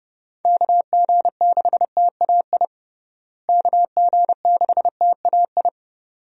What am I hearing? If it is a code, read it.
Morse code, heard as a keyed tone, about 32 wpm, 700 Hz. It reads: KG6TAI KG6TAI
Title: USA Call Signs - 6 Characters - Repeated 2x - 30wpm